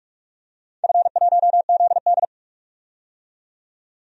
Morse code, heard as a keyed tone, about 47 wpm, 700 Hz. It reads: F1ZD
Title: DXCC-100 Call Signs - 4 Characters - 45wpm